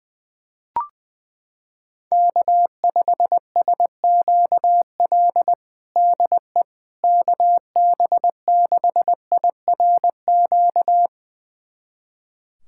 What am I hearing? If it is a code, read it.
Morse code, heard as a keyed tone, about 20 wpm, 700 Hz. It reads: K5SQL DE KB6IRQ
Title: USA Callsign DE Callsign 20wpm